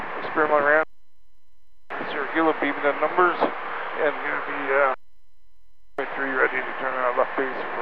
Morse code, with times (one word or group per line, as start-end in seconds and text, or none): none